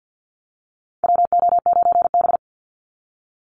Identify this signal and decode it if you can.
Morse code, heard as a keyed tone, about 54 wpm, 700 Hz. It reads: WO9B
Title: Call Signs Repeated 3x - 50wpm